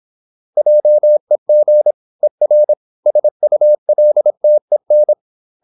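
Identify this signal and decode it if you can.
Morse code, heard as a keyed tone, about 26 wpm, 600 Hz. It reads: JEG ER SULTEN
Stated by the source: High-speed (QRQ) Morse at 26 wpm.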